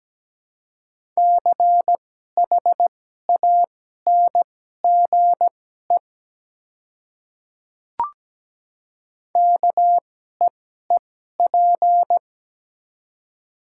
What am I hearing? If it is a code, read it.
Morse code, heard as a keyed tone, about 17 wpm, spaced as for 12 wpm, 700 Hz. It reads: CHANGE KEEP
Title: 2x Character Spacing - Top 300 Words - 17wpm